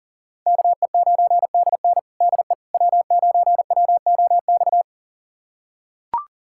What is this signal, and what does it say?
Morse code, heard as a keyed tone, about 40 wpm, 700 Hz. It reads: KE9DN DE W9WOX
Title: USA Callsign DE Callsign - 5 Characters - 40wpm